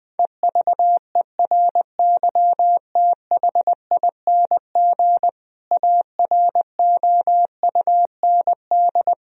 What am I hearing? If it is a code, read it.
Morse code, heard as a keyed tone, about 20 wpm, 700 Hz. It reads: EVERYTHING AROUND
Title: Sentences from Top 1000 Words - Review Part 3 - 20wpm